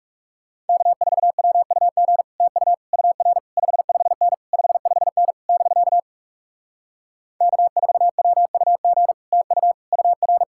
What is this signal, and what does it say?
Morse code, heard as a keyed tone, about 40 wpm, 700 Hz. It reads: K4WUG TU UR 55N 55N <BK> K4WUG TU UR